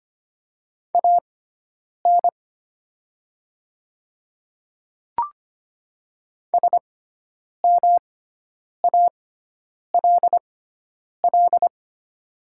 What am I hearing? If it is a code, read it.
Morse code, heard as a keyed tone, about 25 wpm, spaced as for 9 wpm, 700 Hz. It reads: AN SMALL
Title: 6x Character Spacing - Top 200 Words - 25wpm